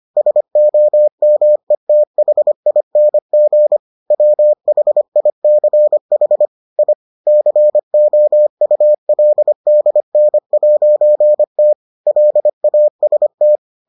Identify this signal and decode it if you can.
Morse code, heard as a keyed tone, about 25 wpm, 600 Hz. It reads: SOMETHING WHICH I COULDN'T LAST